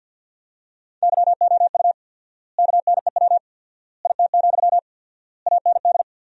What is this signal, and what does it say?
Morse code, heard as a keyed tone, about 50 wpm, 700 Hz. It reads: YOU KNEW IT, AND
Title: 4x Word-Spacing - Sentences from Top 500 Words - 50wpm